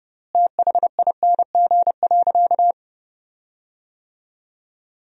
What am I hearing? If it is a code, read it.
Morse code, heard as a keyed tone, about 30 wpm, 700 Hz. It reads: THING.
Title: 3x Word-Spacing - Sentences from Top 200 Words - 30wpm